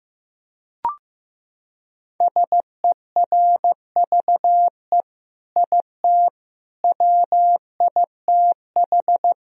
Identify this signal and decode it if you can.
Morse code, heard as a keyed tone, about 15 wpm, 700 Hz. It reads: SERVE IT WITH
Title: Sentences from Top 1000 Words - Review Part 3 - 15wpm